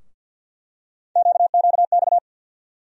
Morse code, heard as a keyed tone, about 47 wpm, 700 Hz. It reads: QQX
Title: USA Call Sign Postfixes - 3 Letters 50wpm